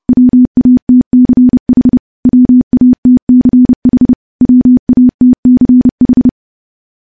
Morse code, heard as a keyed tone, about 30 wpm, 250 Hz. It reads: WATCH WATCH WATCH
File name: morse / morse-training-30WPM